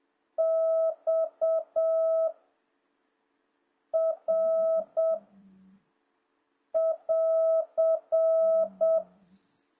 Morse code, heard as a keyed tone, about 7 wpm, 650 Hz. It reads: X R <AR>